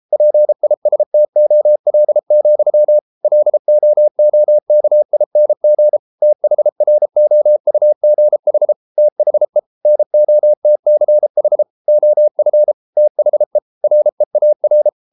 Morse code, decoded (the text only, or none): PISTOL, LOOKING THROUGH THE NOTCH OF THE REAR